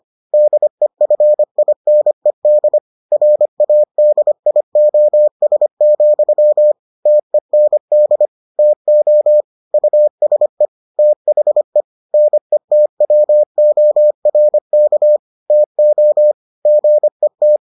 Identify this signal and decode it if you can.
Morse code, heard as a keyed tone, about 25 wpm, 600 Hz. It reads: DEFINED RADIOS, TEND TO USE THE NETWORK TO GET